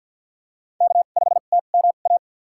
Morse code, heard as a keyed tone, about 51 wpm, spaced as for 36 wpm, 700 Hz. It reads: K5TMA
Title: USA Call Signs - 2x Character Spacing - 50wpm